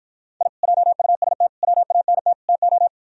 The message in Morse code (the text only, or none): I JUST WANT TO